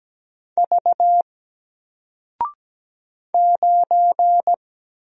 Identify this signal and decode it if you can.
Morse code, heard as a keyed tone, about 17 wpm, 700 Hz. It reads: V 9
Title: Single Letter-Number 17wpm